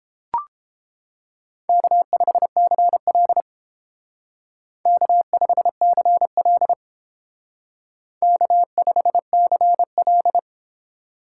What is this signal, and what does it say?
Morse code, heard as a keyed tone, about 31 wpm, 700 Hz. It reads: K5CL K5CL K5CL